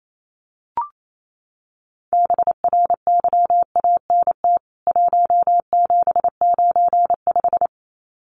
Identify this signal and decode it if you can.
Morse code, heard as a keyed tone, about 28 wpm, 700 Hz. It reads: BRYANT 1795
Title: CWT Exchange 28wpm